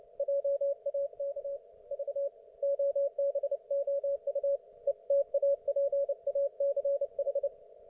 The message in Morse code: JAK V OBOU ETAPACH